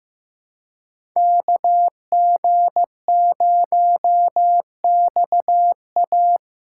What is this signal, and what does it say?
Morse code, heard as a keyed tone, about 15 wpm, 700 Hz. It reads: KG0XA
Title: USA Call Signs - 6x Character Spacing - 15wpm